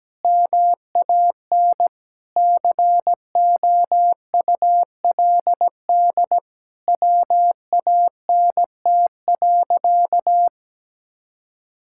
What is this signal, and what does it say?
Morse code, heard as a keyed tone, about 17 wpm, 700 Hz. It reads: MAN COULD WANT.
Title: Sentences from Top 1000 Words - Review Part 5 - 17wpm